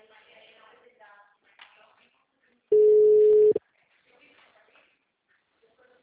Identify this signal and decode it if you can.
Ringback tone (Europe, 425 Hz).
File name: out- -2015